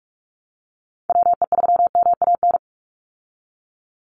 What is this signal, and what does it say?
Morse code, heard as a keyed tone, about 46 wpm, 690 Hz. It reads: WE3MAN
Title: USA Call Signs 45wpm